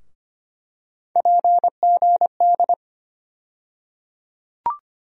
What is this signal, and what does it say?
Morse code, heard as a keyed tone, about 25 wpm, 700 Hz. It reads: PGD
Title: USA Call Sign Postfixes - 3 Letters 25wpm